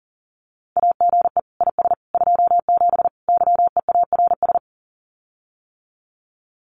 Morse code, heard as a keyed tone, about 40 wpm, 700 Hz. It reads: AGE IS 27 YEARS
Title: Common QSO Phrases Encore 40wpm